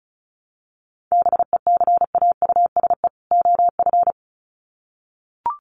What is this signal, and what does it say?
Morse code, heard as a keyed tone, about 34 wpm, 700 Hz. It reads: BECAUSE OF
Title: Sets of 2 Words 35wpm